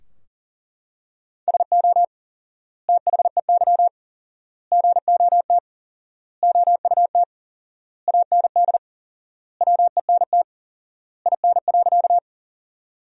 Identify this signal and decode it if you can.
Morse code, heard as a keyed tone, about 40 wpm, 700 Hz. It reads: SO THEY GOT OUT AND WENT IN.
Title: 4x Word-Spacing - Sentences from Top 300 Words - 40wpm